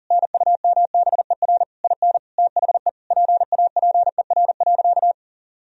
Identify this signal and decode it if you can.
Morse code, heard as a keyed tone, about 40 wpm, 700 Hz. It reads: NUMBER IN THE PAPER.
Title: Sentences from Top 1000 Words - Review Part 1 - 40wpm